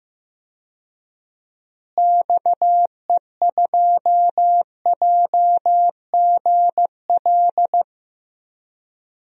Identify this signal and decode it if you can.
Morse code, heard as a keyed tone, about 15 wpm, 700 Hz. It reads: XE2JGL
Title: DXCC-100 Call Signs - 6 Characters - Repeated 3x - 15wpm